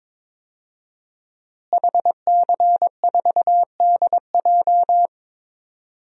Morse code, heard as a keyed tone, about 22 wpm, 700 Hz. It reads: HC4DJ
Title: DXCC-100 Call Signs - 5 Characters - Repeated 3x - 22wpm